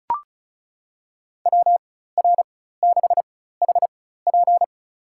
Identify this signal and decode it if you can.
Morse code, heard as a keyed tone, about 34 wpm, spaced as for 16 wpm, 700 Hz. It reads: WR6HP
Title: USA Call Signs - 5 Characters - 4x Character Spacing - 35wpm